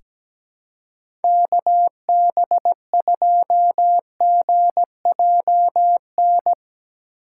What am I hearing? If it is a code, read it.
Morse code, heard as a keyed tone, about 17 wpm, 700 Hz. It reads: KB2GJN
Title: USA Call Signs - 6 Characters - 4x Character Spacing - 17wpm